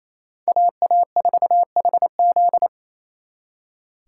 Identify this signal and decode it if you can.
Morse code, heard as a keyed tone, about 35 wpm, 700 Hz. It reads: AA4HZ
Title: Speed Racing - USA 2x2 Call Signs 28wpm